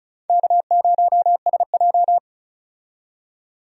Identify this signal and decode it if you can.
Morse code, heard as a keyed tone, about 36 wpm, 700 Hz. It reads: K0SJ
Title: USA Call Signs - 4 Characters - 35wpm